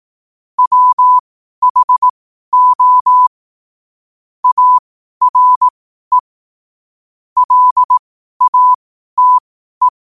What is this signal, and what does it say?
Morse code, heard as a keyed tone, about 18 wpm, spaced as for 12 wpm, 1000 Hz. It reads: WHO ARE LATE
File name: 12wpm